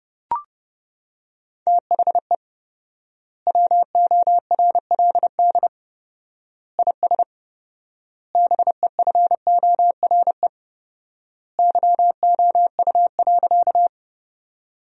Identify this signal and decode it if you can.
Morse code, heard as a keyed tone, about 30 wpm, 700 Hz. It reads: THE WORLD IS BEFORE YOU.
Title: 4x Word-Spacing - Sentences from Top 200 Words - 30wpm